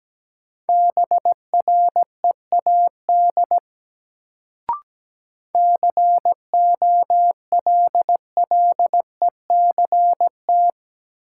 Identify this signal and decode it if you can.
Morse code, heard as a keyed tone, about 17 wpm, 700 Hz. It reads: BREAD COLLECT
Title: Sentences from Top 1600 Words - New Word Drill - 17wpm